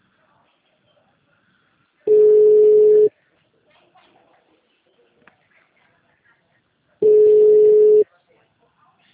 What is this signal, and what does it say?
Ringback tone (Europe, 425 Hz), with speech.